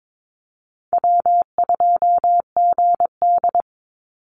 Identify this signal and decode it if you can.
Morse code, heard as a keyed tone, about 22 wpm, 700 Hz. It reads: W2GD
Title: Easy Call Signs - 22wpm